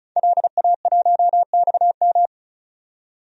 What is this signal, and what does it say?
Morse code, heard as a keyed tone, about 36 wpm, 700 Hz. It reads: LA1XM
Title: DXCC-100 Call Signs - 5 Characters - 35wpm